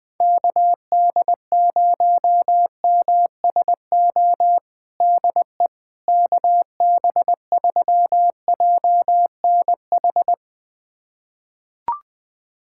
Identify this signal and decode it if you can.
Morse code, heard as a keyed tone, about 20 wpm, 700 Hz. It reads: KD0MSO DE KB3JNH
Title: USA Callsign DE Callsign - 6 Characters - 20wpm